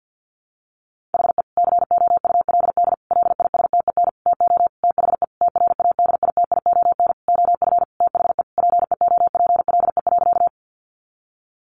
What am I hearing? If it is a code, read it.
Morse code, heard as a keyed tone, about 50 wpm, 700 Hz. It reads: HE COULD LISTEN TO THE TRADITION OF THE PEOPLE.
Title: Sentences from Top 1500 Words 50wpm